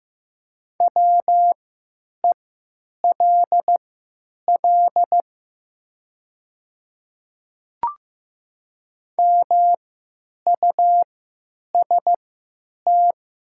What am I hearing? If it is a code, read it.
Morse code, heard as a keyed tone, about 15 wpm, spaced as for 9 wpm, 700 Hz. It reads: WELL MUST
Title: 3x Character Spacing - Top 300 Words - 15wpm